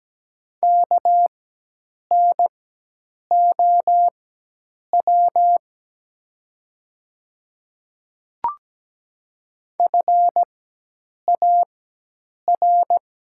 Morse code, heard as a keyed tone, about 17 wpm, spaced as for 8 wpm, 700 Hz. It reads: KNOW FAR
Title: 4x Character Spacing - Top 300 Words - 17wpm